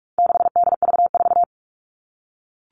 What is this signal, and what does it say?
Morse code, heard as a keyed tone, about 44 wpm, 700 Hz. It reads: 6DV4
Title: All Cut Numbers - 4-Digit 45wpm